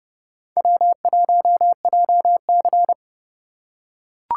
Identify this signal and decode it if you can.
Morse code, heard as a keyed tone, about 29 wpm, 700 Hz. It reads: W1JC